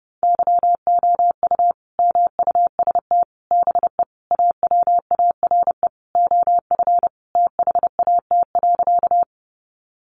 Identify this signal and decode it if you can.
Morse code, heard as a keyed tone, about 30 wpm, 700 Hz. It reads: YOU MUST BE AWARE OF THAT.
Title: Sentences from Top 1200 Words 30wpm